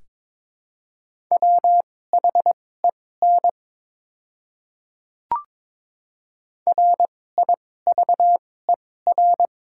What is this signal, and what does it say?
Morse code, heard as a keyed tone, about 22 wpm, spaced as for 16 wpm, 700 Hz. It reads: WHEN RIVER
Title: 2x Character Spacing - Top 500 Words - 22wpm